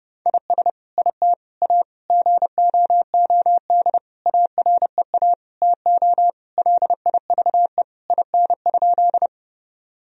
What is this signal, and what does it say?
Morse code, heard as a keyed tone, about 30 wpm, 700 Hz. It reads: IS IT A GOOD AREA TO LIVE IN?
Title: Sentences from Top 1000 Words - Review Part 2 - 30wpm